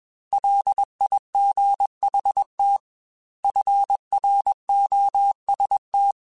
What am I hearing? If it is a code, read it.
Morse code, heard as a keyed tone, about 21 wpm, 800 Hz. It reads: LIGHT FROST
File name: Weather Sets / Weather Sets-mp3s / Weather 4 28 @20